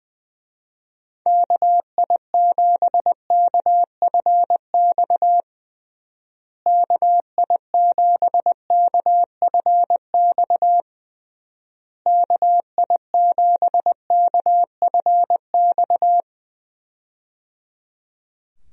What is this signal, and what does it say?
Morse code, heard as a keyed tone, about 20 wpm, 700 Hz. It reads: KI7KFX KI7KFX KI7KFX
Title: USA Call Signs - 6 Characters - Repeated 3x - 20wpm